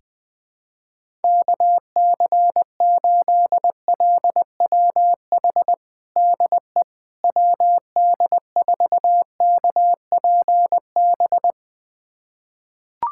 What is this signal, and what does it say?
Morse code, heard as a keyed tone, about 20 wpm, 700 Hz. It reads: KC8LWH DE WD4KPB
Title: USA Callsign DE Callsign 20wpm